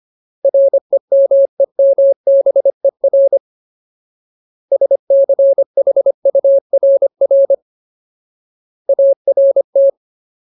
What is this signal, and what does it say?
Morse code, heard as a keyed tone, about 25 wpm, 550 Hz. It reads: REMEMBER SCHURR ART
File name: Ultimate Word List / 25WPM Ultimate